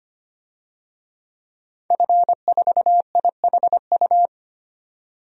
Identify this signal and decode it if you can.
Morse code, heard as a keyed tone, about 25 wpm, 700 Hz. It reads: F4IHU
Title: DXCC-100 Call Signs - 5 Characters - 25wpm